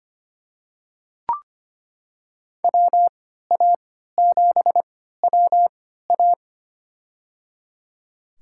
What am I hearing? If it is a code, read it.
Morse code, heard as a keyed tone, about 25 wpm, spaced as for 14 wpm, 700 Hz. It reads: WA7WA
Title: USA Call Signs - 5 Characters - 3x Character Spacing - 25wpm